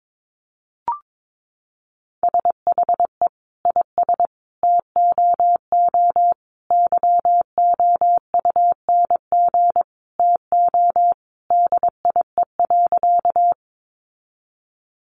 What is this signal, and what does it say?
Morse code, heard as a keyed tone, about 22 wpm, 700 Hz. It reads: SHE IS TOO YOUNG TO DIE.
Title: Sentences from Top 1300 Words 22wpm